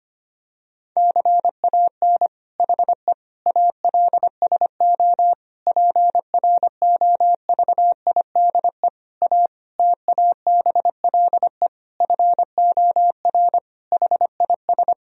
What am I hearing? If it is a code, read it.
Morse code, heard as a keyed tone, about 25 wpm, 700 Hz. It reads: CAN HE ALSO PROVIDE A TABLE FOR HIS
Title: Sentences from Top 1000 Words - Review Part 3 - 25wpm